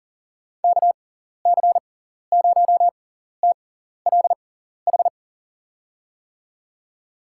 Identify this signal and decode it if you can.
Morse code, heard as a keyed tone, about 38 wpm, spaced as for 14 wpm, 700 Hz. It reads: KC0TLH
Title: USA Call Signs - 6 Characters - 6x Character Spacing - 40wpm